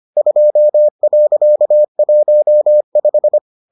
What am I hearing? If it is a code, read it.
Morse code, heard as a keyed tone, about 25 wpm, 600 Hz. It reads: 2.15